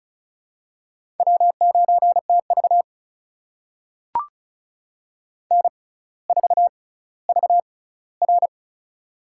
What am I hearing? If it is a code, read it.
Morse code, heard as a keyed tone, about 36 wpm, spaced as for 18 wpm, 700 Hz. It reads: W9TV N4VR
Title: USA Call Signs - 4 Characters - 6x Character Spacing - 35wpm